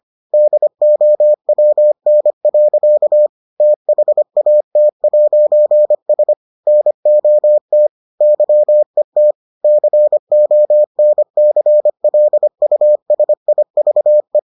Morse code, heard as a keyed tone, about 25 wpm, 600 Hz. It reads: DOWN. THAT'S NOT YET CONCLUSIVE